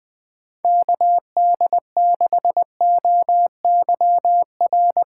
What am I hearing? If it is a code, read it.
Morse code, heard as a keyed tone, about 19 wpm, 700 Hz. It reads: KD6OYR
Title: Speed Racing - USA 2x3 Call Signs 17wpm